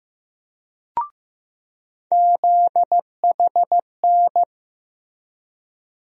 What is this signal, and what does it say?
Morse code, heard as a keyed tone, about 15 wpm, 700 Hz. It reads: ZHN
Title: USA Call Sign Postfixes - 3 Letters 15wpm